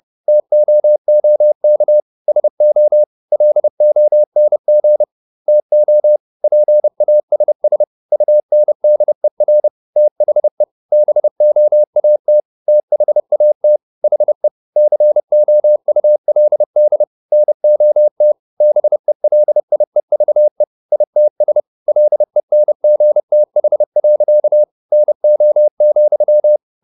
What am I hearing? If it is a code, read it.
Morse code, heard as a keyed tone, about 30 wpm, 600 Hz. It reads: TOOK SO LONG TO PASS UNDER THE BOAT THAT HE COULD NOT BELIEVE ITS LENGTH. NO,